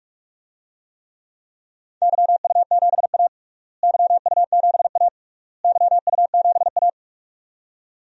Morse code, heard as a keyed tone, about 47 wpm, 700 Hz. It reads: YU7A YU7A YU7A
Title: DXCC-100 Call Signs - 4 Characters - Repeated 3x - 45wpm